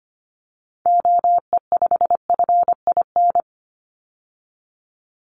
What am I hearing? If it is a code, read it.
Morse code, heard as a keyed tone, about 25 wpm, 700 Hz. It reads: OE5FIN
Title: Call Signs - 25wpm